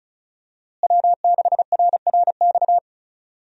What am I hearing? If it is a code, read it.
Morse code, heard as a keyed tone, about 36 wpm, 700 Hz. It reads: W6RRX
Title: USA Call Signs - 5 Characters - Repeated 2x - 35wpm